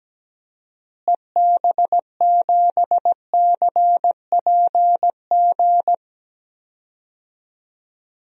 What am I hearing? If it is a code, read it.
Morse code, heard as a keyed tone, about 17 wpm, 700 Hz. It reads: EB7CPG
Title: DXCC-100 Call Signs - 6 Characters - Repeated 2x - 17wpm